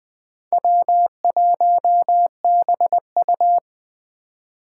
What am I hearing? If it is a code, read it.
Morse code, heard as a keyed tone, about 20 wpm, 700 Hz. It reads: W1BU
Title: Speed Racing - USA 1x2 Call Signs 17wpm